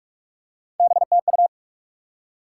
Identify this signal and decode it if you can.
Morse code, heard as a keyed tone, about 46 wpm, 700 Hz. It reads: BTU
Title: Top 90 QSO Elements 45wpm